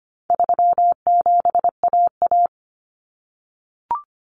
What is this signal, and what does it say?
Morse code, heard as a keyed tone, about 25 wpm, 700 Hz. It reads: 37AA
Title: Common Cut Numbers - 4-Digit 25wpm